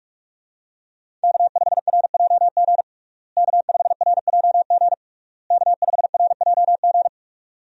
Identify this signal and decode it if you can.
Morse code, heard as a keyed tone, about 41 wpm, 700 Hz. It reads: K5RJG K5RJG K5RJG
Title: USA Call Signs - 5 Characters - Repeated 3x - 45wpm